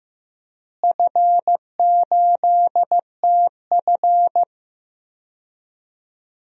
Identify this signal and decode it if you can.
Morse code, heard as a keyed tone, about 15 wpm, 700 Hz. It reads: F8TF
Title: DXCC-100 Call Signs - 4 Characters - Repeated 2x - 15wpm